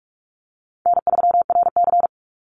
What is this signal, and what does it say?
Morse code, heard as a keyed tone, about 44 wpm, 700 Hz. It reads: N3RC
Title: Call Signs Repeated 3x - 45wpm